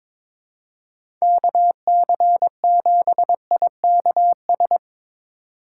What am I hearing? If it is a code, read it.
Morse code, heard as a keyed tone, about 23 wpm, 700 Hz. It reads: KC7IKS
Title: Speed Racing - USA 2x3 Call Signs 22wpm